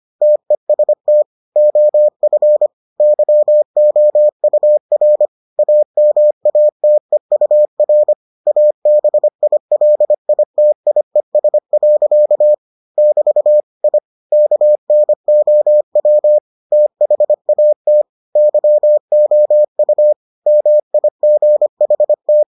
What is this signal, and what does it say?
Morse code, heard as a keyed tone, about 25 wpm, 600 Hz. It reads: TEST OF YOUR AMATEUR ABILITIES. <BT> I KNOW THAT YOU MIGHT